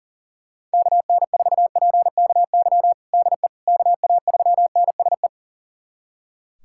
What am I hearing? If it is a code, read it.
Morse code, heard as a keyed tone, about 40 wpm, 700 Hz. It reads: KN4PKY DE KA3NSE
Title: USA Callsign DE Callsign - 6 Characters - 40wpm